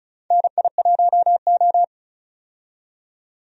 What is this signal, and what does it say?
Morse code, heard as a keyed tone, about 34 wpm, 700 Hz. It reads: NI1O
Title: USA Call Signs - 4 Characters - 35wpm